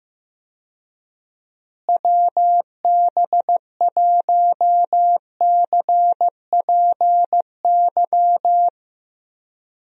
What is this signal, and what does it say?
Morse code, heard as a keyed tone, about 15 wpm, 700 Hz. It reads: WB1CPY
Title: USA Call Signs - 6 Characters - Repeated 2x - 15wpm